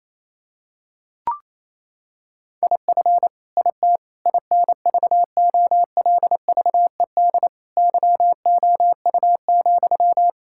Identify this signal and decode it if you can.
Morse code, heard as a keyed tone, about 28 wpm, 700 Hz. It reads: IF IT INVOLVED YOU,
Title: Sentences from Top 1000 Words - Review Part 5 - 28wpm